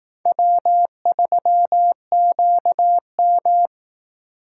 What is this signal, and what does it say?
Morse code, heard as a keyed tone, about 18 wpm, 700 Hz. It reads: W3QM
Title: Speed Racing - USA Call Signs 15wpm